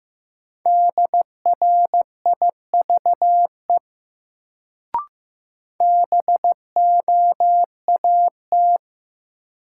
Morse code, heard as a keyed tone, about 15 wpm, 700 Hz. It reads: DRIVE BOAT
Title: Sentences from Top 500 Words - New Word Drill - 15wpm